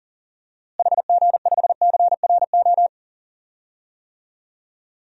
Morse code, heard as a keyed tone, about 37 wpm, 700 Hz. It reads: HG5CRO
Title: DXCC-100 Call Signs - 6 Characters - Repeated 3x - 40wpm